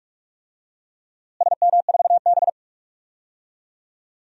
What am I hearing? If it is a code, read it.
Morse code, heard as a keyed tone, about 46 wpm, 700 Hz. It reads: SM4B
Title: DXCC-100 Call Signs - 4 Characters - Repeated 3x - 45wpm